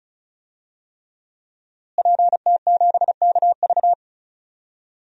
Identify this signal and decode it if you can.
Morse code, heard as a keyed tone, about 36 wpm, 700 Hz. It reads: PT7KV
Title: DXCC-100 Call Signs - 5 Characters - Repeated 3x - 35wpm